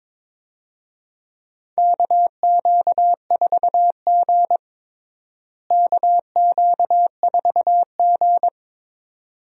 Morse code, heard as a keyed tone, about 22 wpm, 700 Hz. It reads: KQ4G KQ4G
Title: USA Call Signs - 4 Characters - Repeated 2x - 22wpm